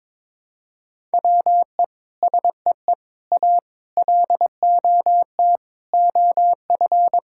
Morse code, heard as a keyed tone, about 22 wpm, 700 Hz. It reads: WE SEE A LOT OF
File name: Sentences with 9-Word Phrases - 22wpm